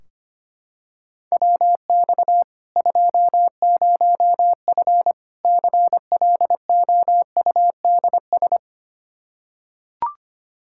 Morse code, heard as a keyed tone, about 25 wpm, 700 Hz. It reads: WX 20F CLOUDS
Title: Practice QSO - Exchange 2 - Info - 25wpm